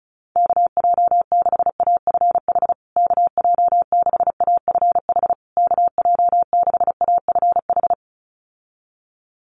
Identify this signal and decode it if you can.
Morse code, heard as a keyed tone, about 34 wpm, 700 Hz. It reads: KJ6AFH KJ6AFH KJ6AFH
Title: USA Call Signs Repeated 3x 35wpm